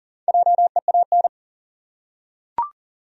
Morse code, heard as a keyed tone, about 39 wpm, 700 Hz. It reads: JEAN
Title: USA Common Names - Max 4 Letters - 40wpm